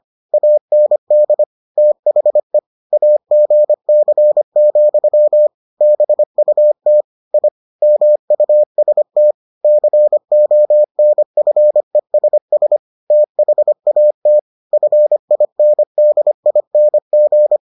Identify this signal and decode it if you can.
Morse code, heard as a keyed tone, about 25 wpm, 600 Hz. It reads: AND THE AGC, BUT I MUST CONFESS THAT FINDING